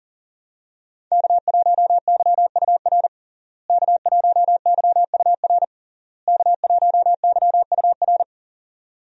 Morse code, heard as a keyed tone, about 37 wpm, 700 Hz. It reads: K1YUR K1YUR K1YUR
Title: USA Call Signs - 5 Characters - Repeated 3x - 40wpm